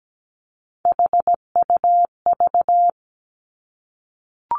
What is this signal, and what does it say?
Morse code, heard as a keyed tone, about 17 wpm, 700 Hz. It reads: HUV
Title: Three Challenging Letter-Number Combinations 17wpm